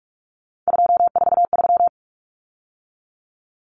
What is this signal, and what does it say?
Morse code, heard as a keyed tone, about 44 wpm, 700 Hz. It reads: 243
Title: Numbers - 3-Digit 45wpm